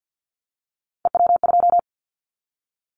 Morse code, heard as a keyed tone, about 50 wpm, 700 Hz. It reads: EW2
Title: Callsign Prefixes for Top 50 DXCC - 50wpm